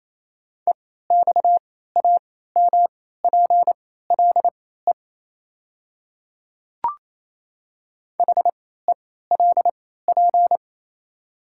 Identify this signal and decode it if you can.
Morse code, heard as a keyed tone, about 28 wpm, spaced as for 16 wpm, 700 Hz. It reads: EXAMPLE HELP
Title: 3x Character Spacing - Top 300 Words - 28wpm